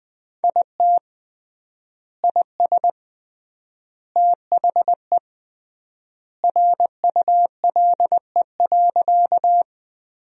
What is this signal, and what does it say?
Morse code, heard as a keyed tone, about 20 wpm, 700 Hz. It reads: IT IS THE RULE.
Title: 3x Word-Spacing - Sentences from Top 400 Words - 20wpm